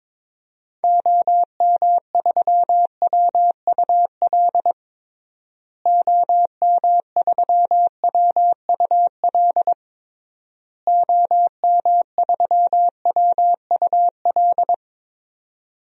Morse code, heard as a keyed tone, about 21 wpm, 700 Hz. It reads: OM3WUL OM3WUL OM3WUL
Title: DXCC-100 Call Signs - 6 Characters - Repeated 3x - 22wpm